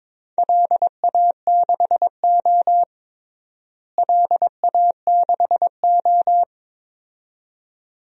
Morse code, heard as a keyed tone, about 22 wpm, 700 Hz. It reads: LA6O LA6O
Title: DXCC-100 Call Signs - 4 Characters - Repeated 2x - 22wpm